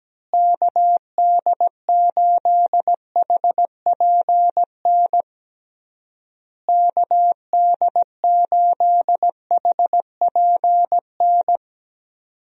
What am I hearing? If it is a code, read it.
Morse code, heard as a keyed tone, about 17 wpm, 700 Hz. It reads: KD8HPN KD8HPN
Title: USA Call Signs - 6 Characters - Repeated 3x - 17wpm